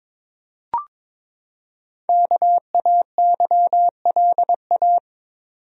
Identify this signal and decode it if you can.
Morse code, heard as a keyed tone, about 22 wpm, 700 Hz. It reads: KAYLA
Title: USA Common Names - Max 5 Letters - 22wpm